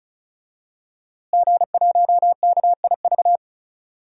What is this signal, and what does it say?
Morse code, heard as a keyed tone, about 36 wpm, 700 Hz. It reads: G1KIV
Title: DXCC-100 Call Signs - 5 Characters - Repeated 2x - 35wpm